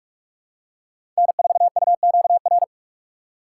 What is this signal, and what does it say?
Morse code, heard as a keyed tone, about 46 wpm, 700 Hz. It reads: N4UQR
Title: USA Call Signs - 4x Character Spacing - 45wpm